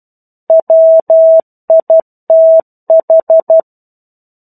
Morse code, heard as a keyed tone, about 12 wpm, 650 Hz. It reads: WITH
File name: 01-Chapter-01 12 WPM